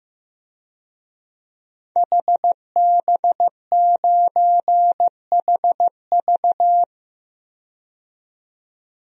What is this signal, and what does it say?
Morse code, heard as a keyed tone, about 15 wpm, 700 Hz. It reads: HB9HV
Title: DXCC-100 Call Signs - 5 Characters - 15wpm